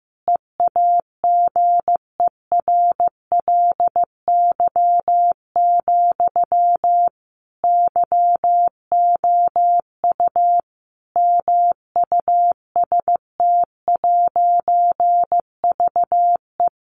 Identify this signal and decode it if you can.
Morse code, heard as a keyed tone, about 15 wpm, 700 Hz. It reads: EAGERLY, YOU MUST'VE
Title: A Sound Of Thunder 15wpm